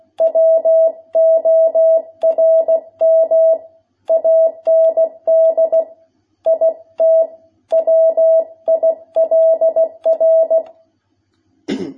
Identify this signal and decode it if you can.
Morse code, heard as a keyed tone, about 16 wpm, 660 Hz. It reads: WORM AND IT WILR
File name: jh8cbh / 2021cw